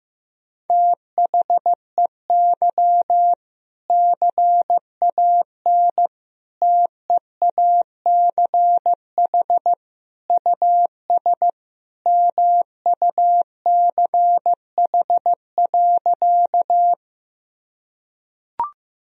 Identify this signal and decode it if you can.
Morse code, heard as a keyed tone, about 16 wpm, 700 Hz. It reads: THEY CAN TEACH US MUCH.
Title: Sentences from Top 1000 Words - Review Part 2 - 15wpm